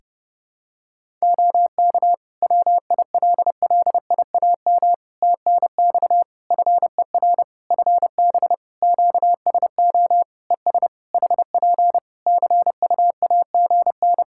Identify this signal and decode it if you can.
Morse code, heard as a keyed tone, about 30 wpm, 700 Hz. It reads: OK WILLIAM TNX FER FB QSO ES HP CUAGN
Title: Practice QSO - Exchange 4 - Ending - 30wpm